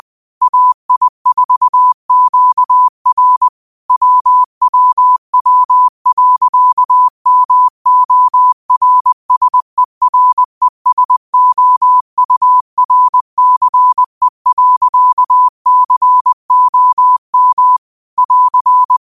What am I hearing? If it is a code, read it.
Morse code, heard as a keyed tone, about 20 wpm, 1000 Hz. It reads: AI4QR WWW.MORSERESOURCE.COM <AR>
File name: qotdfiles / 20wpm